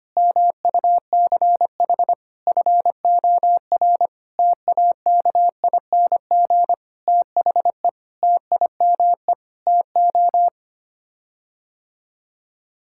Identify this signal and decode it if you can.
Morse code, heard as a keyed tone, about 25 wpm, 700 Hz. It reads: MUCH FOR TAKING THE TIME TO
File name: Sets of 9 Words - 25wpm